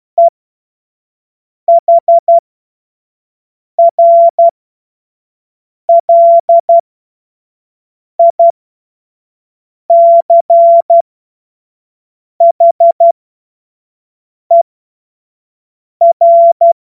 Morse code, heard as a keyed tone, about 12 wpm, spaced as for 5 wpm, 680 Hz. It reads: EHRLICHER
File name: Morsekurs / Witze / Witz 07. Juni 2022 25 F70